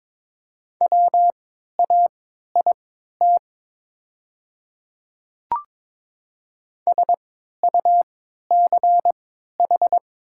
Morse code, heard as a keyed tone, about 23 wpm, spaced as for 13 wpm, 690 Hz. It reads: WAIT SUCH
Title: 3x Character Spacing - Top 500 Words - 22wpm